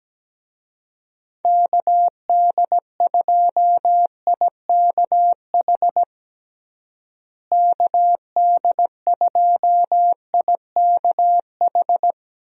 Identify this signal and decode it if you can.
Morse code, heard as a keyed tone, about 17 wpm, 700 Hz. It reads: KD2IKH KD2IKH
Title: USA Call Signs - 6 Characters - Repeated 2x - 17wpm